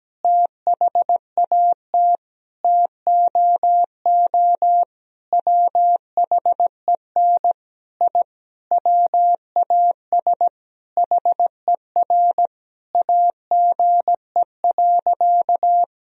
Morse code, heard as a keyed tone, about 17 wpm, 700 Hz. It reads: THAT TOO WHEN I WAS HER AGE.
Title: Sentences from Top 1000 Words - Review Part 3 - 17wpm